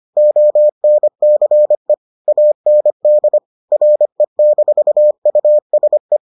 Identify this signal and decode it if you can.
Morse code, heard as a keyed tone, about 25 wpm, 600 Hz. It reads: ONCE AND RE-USE